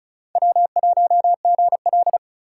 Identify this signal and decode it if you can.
Morse code, heard as a keyed tone, about 36 wpm, 690 Hz. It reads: W1GL
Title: USA Call Signs - 4 Characters - 2x Character Spacing - 35wpm